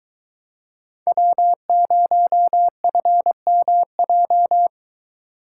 Morse code, heard as a keyed tone, about 21 wpm, 700 Hz. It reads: W0FMJ